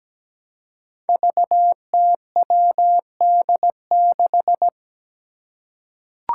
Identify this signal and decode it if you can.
Morse code, heard as a keyed tone, about 17 wpm, 700 Hz. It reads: VTWD6
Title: Five Challenging Letter-Number Combinations 17wpm